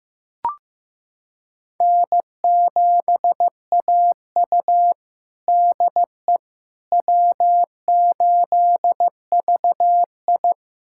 Morse code, heard as a keyed tone, about 16 wpm, 700 Hz. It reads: N7AU DE W8VI
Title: USA Callsign DE Callsign - 4 Characters - 15wpm